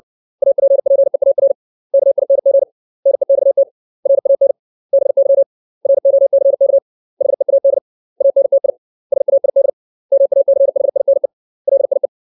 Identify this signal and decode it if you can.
Morse code, heard as a keyed tone, about 60 wpm, 550 Hz. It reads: APPEAR KING NEXT RAN BY ROCK HEAD ANTI SAID MACHINE LIE
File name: 632 words / 632 Words 60WPM